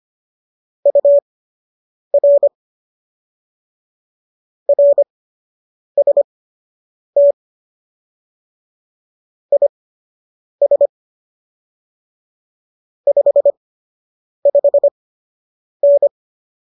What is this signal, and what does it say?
Morse code, heard as a keyed tone, about 25 wpm, spaced as for 8 wpm, 570 Hz. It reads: UR RST IS 55N